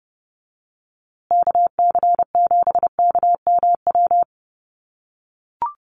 Morse code, heard as a keyed tone, about 30 wpm, 700 Hz. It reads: KC7KMW
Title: USA Call Signs Repeated 3x 30wpm